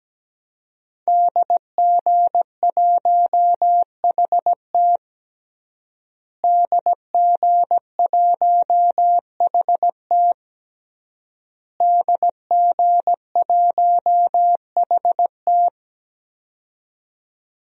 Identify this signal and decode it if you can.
Morse code, heard as a keyed tone, about 17 wpm, 700 Hz. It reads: DG1HT DG1HT DG1HT
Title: DXCC-100 Call Signs - 5 Characters - Repeated 3x - 17wpm